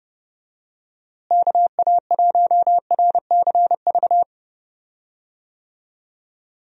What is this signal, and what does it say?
Morse code, heard as a keyed tone, about 32 wpm, 700 Hz. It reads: KA1RCV
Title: USA Call Signs - 6 Characters - Repeated 2x - 30wpm